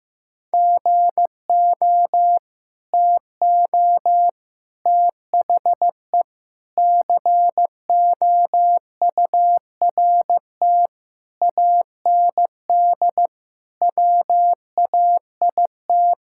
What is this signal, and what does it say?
Morse code, heard as a keyed tone, about 15 wpm, 700 Hz. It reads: GO TO THE COURT AND WAIT
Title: Sentences from Top 1600 Words 15wpm